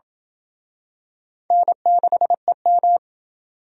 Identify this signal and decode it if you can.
Morse code, heard as a keyed tone, about 27 wpm, 700 Hz. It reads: N6EM